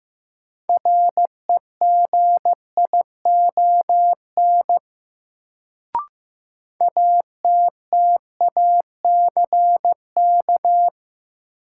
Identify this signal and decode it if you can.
Morse code, heard as a keyed tone, about 15 wpm, 700 Hz. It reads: REGION ATTACK
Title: Sentences from Top 1500 Words - New Word Drill Alt - 15wpm